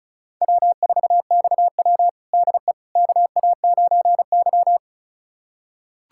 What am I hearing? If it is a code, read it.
Morse code, heard as a keyed tone, about 34 wpm, 700 Hz. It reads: W4XW DE KA9Y
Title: USA Callsign DE Callsign - 4 Characters - 35wpm